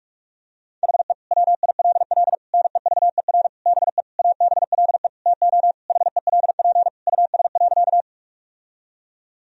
Morse code, heard as a keyed tone, about 46 wpm, 700 Hz. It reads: HE WILL NEVER BE ABLE TO HELP US.
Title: Sentences from Top 1000 Words - Review Part 2 - 45wpm